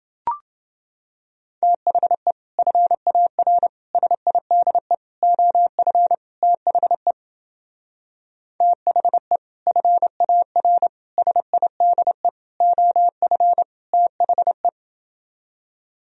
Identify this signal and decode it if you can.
Morse code, heard as a keyed tone, about 28 wpm, 700 Hz. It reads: THE FAR SIDE OF THE THE FAR SIDE OF THE